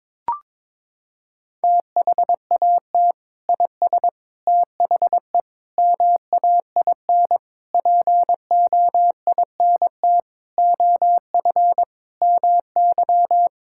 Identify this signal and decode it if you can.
Morse code, heard as a keyed tone, about 22 wpm, 700 Hz. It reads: THAT IS THE MAIN POINT OF MY
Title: Sentences from Top 1000 Words - Review Part 5 - 22wpm